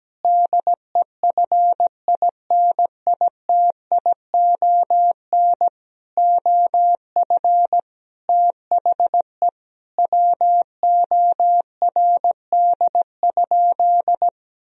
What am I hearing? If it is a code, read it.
Morse code, heard as a keyed tone, about 17 wpm, 700 Hz. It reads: DEFINITION OF THE WORD?
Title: Sentences from Top 1000 Words - Review Part 4 - 17wpm